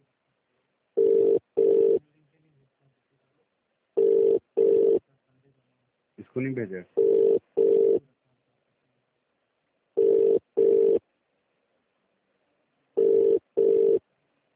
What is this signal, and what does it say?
Ringback tone (UK), 400 + 450 Hz, with speech.